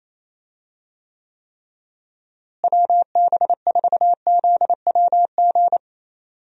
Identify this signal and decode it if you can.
Morse code, heard as a keyed tone, about 28 wpm, 700 Hz. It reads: WB4ZWG
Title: USA Call Signs - 6 Characters - Repeated 3x - 28wpm